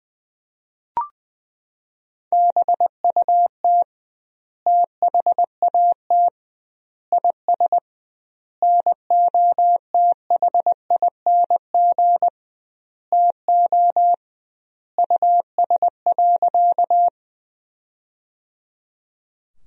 Morse code, heard as a keyed tone, about 20 wpm, 700 Hz. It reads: BUT THAT IS NOTHING TO US.
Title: 2x Word-Spacing - Sentences from Top 500 Words - 20wpm